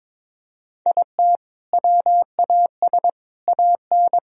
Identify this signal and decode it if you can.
Morse code, heard as a keyed tone, about 22 wpm, 700 Hz. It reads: IT WAS AN
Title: Sentences from Top 1000 Words - Review Part 4 - 22wpm